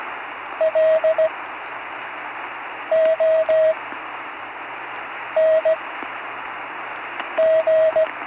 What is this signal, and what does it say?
Morse code, heard as a keyed tone, about 17 wpm, spaced as for 5 wpm, 640 Hz, with radio noise. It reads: LONG
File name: CW Learning Programs / Learning Morse Code (from Tune in the World) / 35 answers